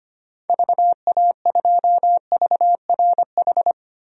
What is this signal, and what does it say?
Morse code, heard as a keyed tone, about 25 wpm, 700 Hz. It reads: VA2VRH
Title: DXCC-100 Call Signs - 6 Characters - Repeated 3x - 25wpm